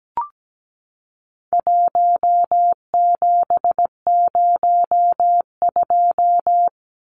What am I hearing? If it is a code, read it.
Morse code, heard as a keyed tone, about 17 wpm, 700 Hz. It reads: 1702
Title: Numbers - 4-Digit 17wpm